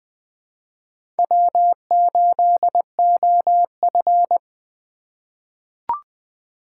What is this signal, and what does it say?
Morse code, heard as a keyed tone, about 20 wpm, 700 Hz. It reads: W8OF
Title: USA Call Signs - 4 Characters - 4x Character Spacing - 20wpm